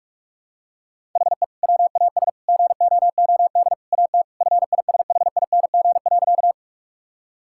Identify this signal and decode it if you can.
Morse code, heard as a keyed tone, about 46 wpm, 700 Hz. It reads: HE WAS GOOD AT FISHING.
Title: Sentences from Top 1000 Words - Review Part 5 - 45wpm